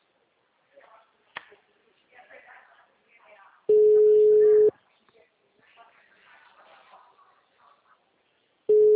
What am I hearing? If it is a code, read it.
Ringback tone (Europe, 425 Hz), with speech.